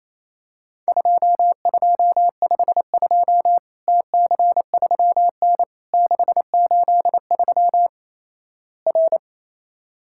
Morse code, heard as a keyed tone, about 28 wpm, 700 Hz. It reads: 2252 TC3N 683 R
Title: Worked All Europe DX Contest - QTC Exchange - 28wpm